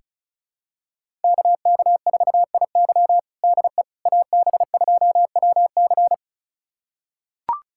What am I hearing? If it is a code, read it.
Morse code, heard as a keyed tone, about 36 wpm, 700 Hz. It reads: KK4IY DE AB2WC
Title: USA Callsign DE Callsign - 5 Characters - 35wpm